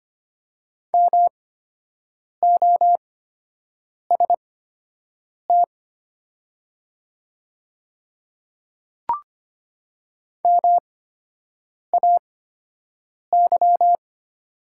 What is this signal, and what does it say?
Morse code, heard as a keyed tone, about 25 wpm, spaced as for 7 wpm, 700 Hz. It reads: MOST MAY
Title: 8x Character Spacing - Top 100 Words - 25wpm